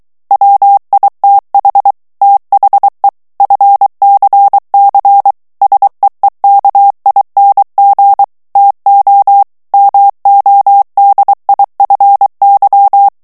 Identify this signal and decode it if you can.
Morse code, heard as a keyed tone, about 23 wpm, 800 Hz. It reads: WITH THE FCC SEEKING TO MODIFY
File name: ANGLAIS texte clair 3 vitesse 22 mots minute